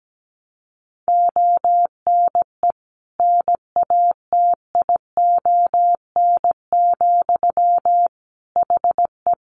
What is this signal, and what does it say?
Morse code, heard as a keyed tone, about 17 wpm, 700 Hz. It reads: ONE NATION, HE
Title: Sentences from Top 900 Words 17wpm